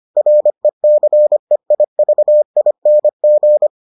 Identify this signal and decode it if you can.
Morse code, heard as a keyed tone, about 25 wpm, 600 Hz. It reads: RECEIVING